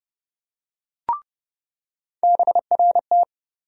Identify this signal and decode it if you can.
Morse code, heard as a keyed tone, about 30 wpm, 700 Hz. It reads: BRT
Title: USA Call Sign Postfixes - 3 Letters 30wpm